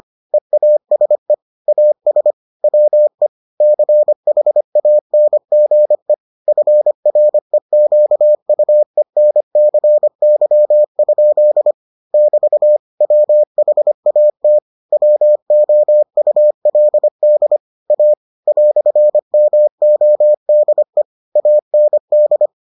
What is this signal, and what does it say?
Morse code, heard as a keyed tone, about 25 wpm, 600 Hz. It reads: EASE AS WE CHANGE FREQUENCY? <BT> WHAT WOULD A "MODE AND